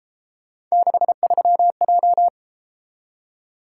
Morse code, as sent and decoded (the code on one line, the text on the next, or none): -.... ...-- .---
63J